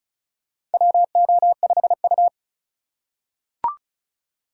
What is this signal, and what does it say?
Morse code, heard as a keyed tone, about 34 wpm, 700 Hz. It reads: WO5U
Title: USA Call Signs - 4 Characters - 4x Character Spacing - 35wpm